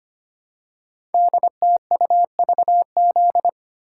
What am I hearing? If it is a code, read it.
Morse code, heard as a keyed tone, about 25 wpm, 700 Hz. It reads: DTUVZ
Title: Code Groups - 5 Letters - 25wpm